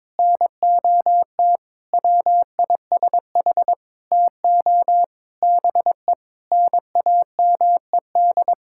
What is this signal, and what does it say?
Morse code, heard as a keyed tone, about 22 wpm, 700 Hz. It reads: NOT WISH TO BE NAMED
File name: Sentences with 7-Word Phrases - 22wpm